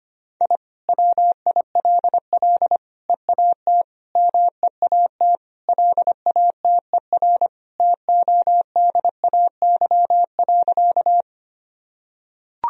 Morse code, heard as a keyed tone, about 25 wpm, 700 Hz. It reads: I WILL EAT MEAT LATER TODAY.
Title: Sentences from Top 1000 Words - Review Part 3 - 25wpm